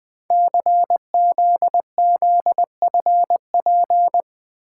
Morse code, heard as a keyed tone, about 20 wpm, 700 Hz. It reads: CZZFP
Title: Code Groups - 5 Letter:Numbers - 20wpm